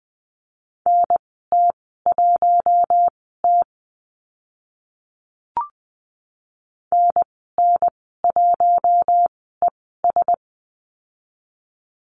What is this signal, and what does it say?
Morse code, heard as a keyed tone, about 20 wpm, spaced as for 14 wpm, 700 Hz. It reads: NT1T NN1ES
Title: 042 Lesson 8g - Number 1 - Call Signs 20wpm